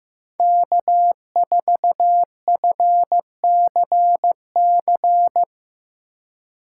Morse code, heard as a keyed tone, about 15 wpm, 700 Hz. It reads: K4FCC
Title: K1USN SST QSO Practice - 15wpm